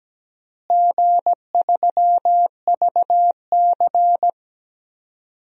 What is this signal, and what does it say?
Morse code, heard as a keyed tone, about 17 wpm, 700 Hz. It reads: G3VC
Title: DXCC-100 Call Signs - 4 Characters - Repeated 2x - 17wpm